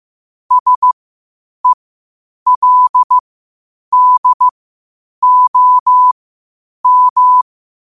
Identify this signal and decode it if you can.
Morse code, heard as a keyed tone, about 15 wpm, spaced as for 8 wpm, 1000 Hz. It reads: SELDOM
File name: qotdfiles / 8wpm